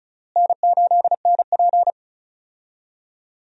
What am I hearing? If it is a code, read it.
Morse code, heard as a keyed tone, about 34 wpm, 700 Hz. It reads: N8NP
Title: USA Call Signs - 4 Characters - Repeated 3x - 35wpm